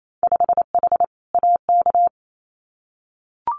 Morse code, heard as a keyed tone, about 28 wpm, 700 Hz. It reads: 5H AK
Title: Winter Field Day Exchanges Encore 28wpm